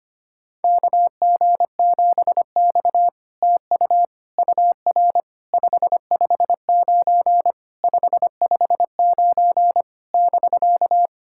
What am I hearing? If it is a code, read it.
Morse code, heard as a keyed tone, about 25 wpm, 700 Hz. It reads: KG7X TU UR 559 559 <BK>
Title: SOTA QSO Practice 25wpm